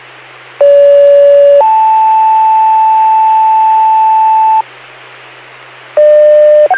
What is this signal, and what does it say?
Two-tone page, heard as a one-second tone, then a three-second tone, about 570 then 880 Hz. It is heard from 0.6 to 4.6 s.